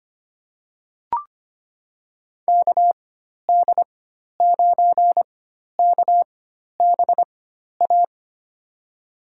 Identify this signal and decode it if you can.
Morse code, heard as a keyed tone, about 25 wpm, spaced as for 12 wpm, 700 Hz. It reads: KD9KBA
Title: USA Call Signs - 4x Character Spacing - 25wpm